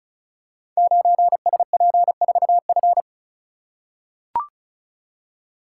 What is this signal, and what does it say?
Morse code, heard as a keyed tone, about 36 wpm, 700 Hz. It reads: 9SP4F
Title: Code Groups - 5 Letter:Numbers - 35wpm